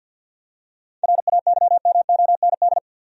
Morse code, heard as a keyed tone, about 53 wpm, 700 Hz. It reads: RAYMOND
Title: USA Common Names - 50wpm